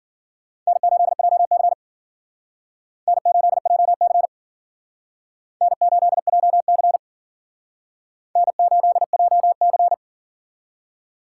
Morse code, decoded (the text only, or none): N8JC N8JC N8JC N8JC